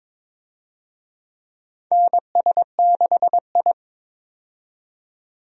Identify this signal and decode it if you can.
Morse code, heard as a keyed tone, about 22 wpm, 700 Hz. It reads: NS6I
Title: USA Call Signs - 4 Characters - 22wpm